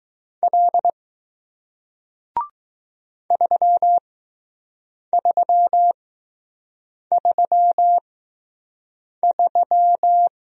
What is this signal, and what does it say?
Morse code, heard as a keyed tone, about 20 wpm, spaced as for 6 wpm, 700 Hz. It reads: L 3333